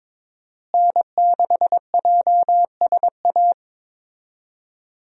Morse code, heard as a keyed tone, about 21 wpm, 700 Hz. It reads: N6JSA
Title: USA Call Signs - 5 Characters - Repeated 3x - 22wpm